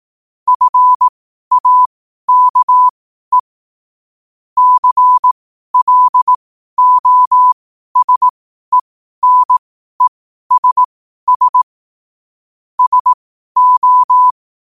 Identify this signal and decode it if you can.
Morse code, heard as a keyed tone, about 18 wpm, spaced as for 12 wpm, 1000 Hz. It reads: FAKE CLOSENESS SO
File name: qotdfiles / 12wpm